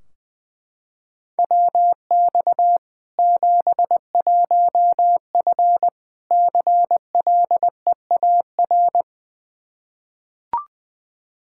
Morse code, heard as a keyed tone, about 20 wpm, 700 Hz. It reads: WX 71F CLEAR
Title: K1USN SST QSO Practice 20wpm